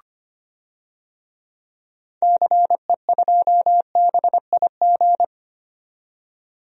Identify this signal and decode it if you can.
Morse code, heard as a keyed tone, about 25 wpm, 700 Hz. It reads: CE2BIG
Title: DXCC-100 Call Signs - 6 Characters - 25wpm